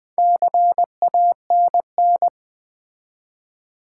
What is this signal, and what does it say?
Morse code, heard as a keyed tone, about 20 wpm, 690 Hz. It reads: CANN
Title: Build A Word - 2 Letters At A Time - 6-Letter Words - 20wpm